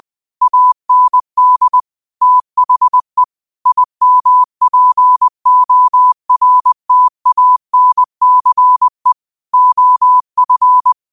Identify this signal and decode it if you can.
Morse code, heard as a keyed tone, about 20 wpm, 1000 Hz. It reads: AND THE IMPORTANCE OF
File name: qotdfiles / 20wpm